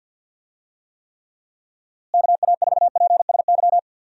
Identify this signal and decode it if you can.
Morse code, heard as a keyed tone, about 49 wpm, 700 Hz. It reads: KA4PSY
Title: USA Call Signs - 6 Characters - Repeated 2x - 50wpm